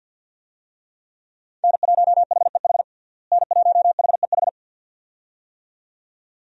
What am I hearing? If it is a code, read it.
Morse code, heard as a keyed tone, about 51 wpm, 700 Hz. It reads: N1HEH N1HEH
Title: USA Call Signs - 5 Characters - Repeated 2x - 50wpm